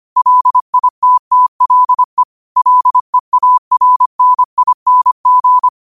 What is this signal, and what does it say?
Morse code, heard as a keyed tone, about 25 wpm, 1000 Hz. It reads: LITTLE LEARNING
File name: qotdfiles / 25wpm